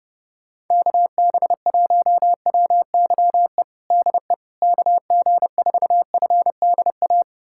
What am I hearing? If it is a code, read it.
Morse code, heard as a keyed tone, about 30 wpm, 700 Hz. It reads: KB1WYE DE KG4FDA
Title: USA Callsign DE Callsign - 6 Characters - 30wpm